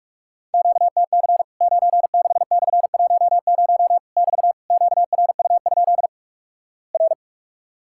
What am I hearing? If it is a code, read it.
Morse code, heard as a keyed tone, about 46 wpm, 700 Hz. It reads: QTC 96/10 <BT> QRU? R
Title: Worked All Europe DX Contest - QTC Exchange - 45wpm